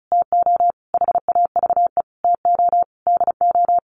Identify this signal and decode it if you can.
Morse code, heard as a keyed tone, about 34 wpm, 700 Hz. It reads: TO HAVE TO DO
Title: Sets of 5 Words Encore 35wpm